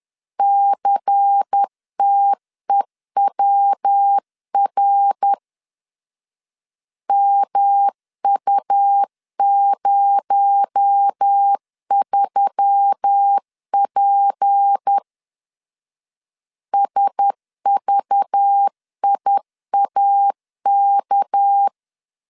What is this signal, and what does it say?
Morse code, heard as a keyed tone, about 10 wpm, 800 Hz. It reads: CTEWR MU03P SVIAK
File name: LEZIONI / 6 PKX0 / 6 PKX0 10wpm 2707